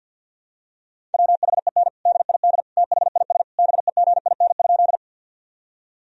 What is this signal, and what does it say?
Morse code, heard as a keyed tone, about 50 wpm, 700 Hz. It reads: WHEN DID THIS BEGIN?
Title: Sentences from Top 1000 Words - Review Part 1 - 50wpm